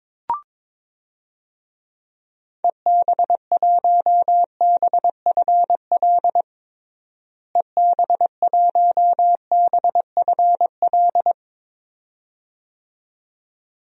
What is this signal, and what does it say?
Morse code, heard as a keyed tone, about 22 wpm, 700 Hz. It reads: EB1BFL EB1BFL
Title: DXCC-100 Call Signs - 6 Characters - Repeated 2x - 22wpm